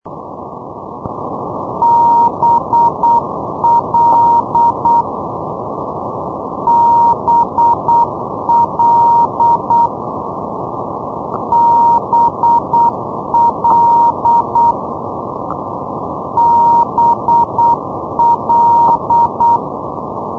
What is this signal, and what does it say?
Morse code, heard as a keyed tone, about 8 wpm, 940 Hz, with noise. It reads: BL BL BL BL
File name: europe / isl / isl-203 ndb blonduos bl-351-isl-2012-04-rb